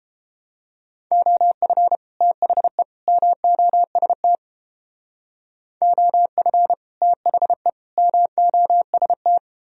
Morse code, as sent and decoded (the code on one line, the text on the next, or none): --- ..-. / - .... / -- --- ... - / --- ..-. / - .... / -- --- ... -
OF THE MOST OF THE MOST